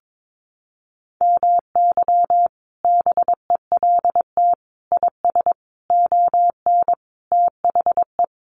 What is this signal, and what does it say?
Morse code, heard as a keyed tone, about 22 wpm, 700 Hz. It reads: MY BELT IS ON THE
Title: Sentences from Top 1400 Words 22wpm